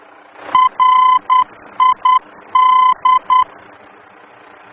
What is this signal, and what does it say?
Morse code, heard as a keyed tone, about 10 wpm, 1020 Hz. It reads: RID
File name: germany / mx / RID-D-VOR